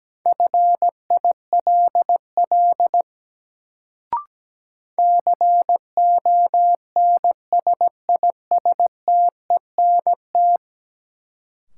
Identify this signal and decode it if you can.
Morse code, heard as a keyed tone, about 17 wpm, 700 Hz. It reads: FILL CONSISTENT
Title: Sentences from Top 1500 Words - New Word Drill Alt - 17wpm